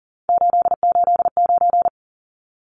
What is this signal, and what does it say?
Morse code, heard as a keyed tone, about 40 wpm, 700 Hz. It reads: 889
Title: Numbers - 3-Digit 40wpm